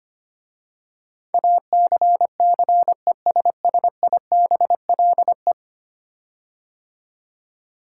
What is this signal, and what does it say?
Morse code, heard as a keyed tone, about 25 wpm, 700 Hz. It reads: ACCESSIBLE
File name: 10 Letter Words - First 3 Letters Given - 25wpm